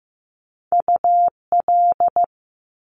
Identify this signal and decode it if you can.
Morse code, heard as a keyed tone, about 15 wpm, 700 Hz. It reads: UL
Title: Top Two Letter Combinations in Words 15wpm